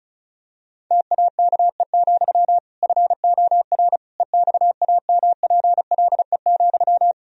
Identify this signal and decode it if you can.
Morse code, heard as a keyed tone, about 36 wpm, 700 Hz. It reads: TAKE, FOR EXAMPLE,
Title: Sentences from Top 1000 Words - Review Part 1 - 35wpm